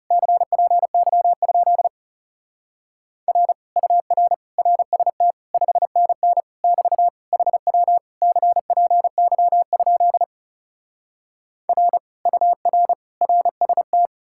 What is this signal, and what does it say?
Morse code, heard as a keyed tone, about 37 wpm, 700 Hz. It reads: CPY? R UR RST 5NN <BT> HW CPY? R UR RST
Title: Speed-Racing Common QSO Phrases Encore - 30wpm